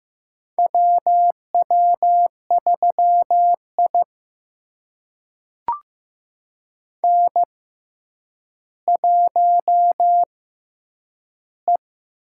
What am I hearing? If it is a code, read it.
Morse code, heard as a keyed tone, about 15 wpm, spaced as for 9 wpm, 700 Hz. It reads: WW3I N1E
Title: USA Call Signs - 4 Characters - 6x Character Spacing - 15wpm